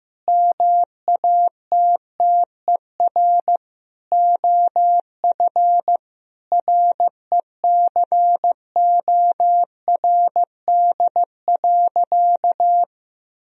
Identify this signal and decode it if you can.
Morse code, heard as a keyed tone, about 15 wpm, 700 Hz. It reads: MATTER OF RECORD.
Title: Sentences from Top 1000 Words - Review Part 3 - 15wpm